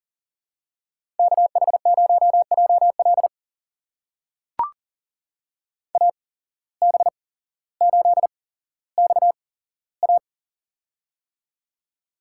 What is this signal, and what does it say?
Morse code, heard as a keyed tone, about 38 wpm, spaced as for 17 wpm, 700 Hz. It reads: KH0JL AB8XA
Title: USA Call Signs - 8x Character Spacing - 40wpm